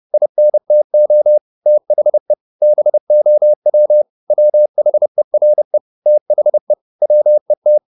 Morse code, heard as a keyed tone, about 30 wpm, 600 Hz. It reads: INTO THE BOW WHERE THE WET
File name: the old man and the sea / the old man and the sea 1 30wpm